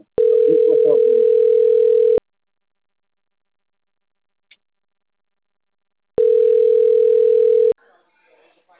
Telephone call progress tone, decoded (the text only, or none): ringback tone